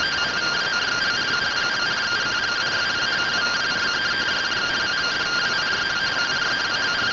Two-tone teleprinter signal, about 170 Hz shift.